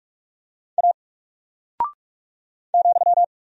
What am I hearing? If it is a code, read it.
Morse code, heard as a keyed tone, about 44 wpm, 700 Hz. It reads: A ,
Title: Single Character (Letter:Number:Punctuation:Prosign) Alt 45wpm